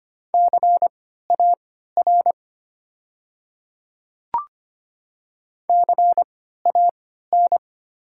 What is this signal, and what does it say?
Morse code, heard as a keyed tone, about 25 wpm, spaced as for 14 wpm, 700 Hz. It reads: CAR CAN
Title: 3x Character Spacing - Top 300 Words - 25wpm